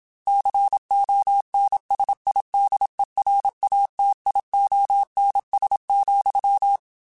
Morse code, heard as a keyed tone, about 27 wpm, 800 Hz. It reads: CONSIDERATIONS,
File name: Technical Subjects / Technical Subjects-mp3s / Sw Pwr Sup @25